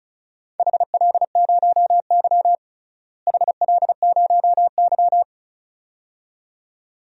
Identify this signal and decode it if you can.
Morse code, heard as a keyed tone, about 34 wpm, 700 Hz. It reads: HL0Y HL0Y
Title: DXCC-100 Call Signs - 4 Characters - Repeated 2x - 35wpm